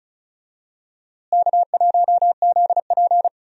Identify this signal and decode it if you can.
Morse code, heard as a keyed tone, about 34 wpm, 700 Hz. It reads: K1ZP
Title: USA Call Signs - 4 Characters - Repeated 3x - 35wpm